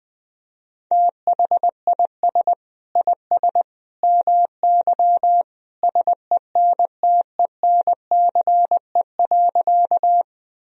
Morse code, heard as a keyed tone, about 21 wpm, 700 Hz. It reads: THIS IS MY SENTENCE.
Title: Sentences from Top 1000 Words - Review Part 1 - 20wpm